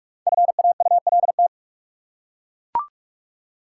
Morse code, heard as a keyed tone, about 41 wpm, 700 Hz. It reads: FAULT
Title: Five Letter Words - 45wpm